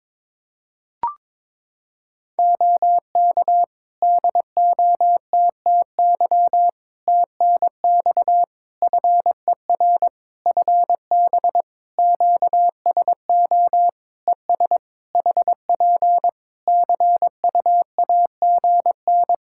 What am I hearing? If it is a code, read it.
Morse code, heard as a keyed tone, about 22 wpm, 700 Hz. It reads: OK DOTTY TNX FER FB QSO ES HP CUAGN
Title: Practice QSO - Exchange 4 - Ending - 22wpm